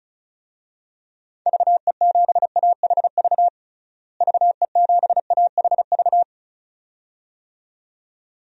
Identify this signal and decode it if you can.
Morse code, heard as a keyed tone, about 36 wpm, 700 Hz. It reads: VE7AHV VE7AHV
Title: DXCC-100 Call Signs - 6 Characters - Repeated 2x - 35wpm